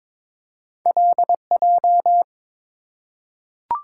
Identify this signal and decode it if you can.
Morse code, heard as a keyed tone, about 22 wpm, 700 Hz. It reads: LJ
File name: practice / 22wpm / Two Letter-Number Combinations - 22wpm